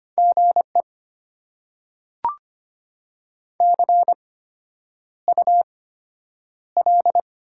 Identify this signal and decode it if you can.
Morse code, heard as a keyed tone, about 25 wpm, spaced as for 9 wpm, 700 Hz. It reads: GE CUL
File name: Top 90 QSO Elements - 8x Character Spacing - 25wpm